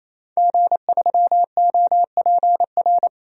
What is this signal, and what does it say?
Morse code, heard as a keyed tone, about 28 wpm, 700 Hz. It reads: G3OPR
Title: DXCC-100 Call Signs - 5 Characters - 28wpm